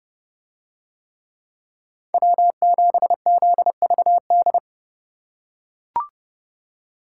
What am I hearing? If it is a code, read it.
Morse code, heard as a keyed tone, about 31 wpm, 700 Hz. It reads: W7ZVD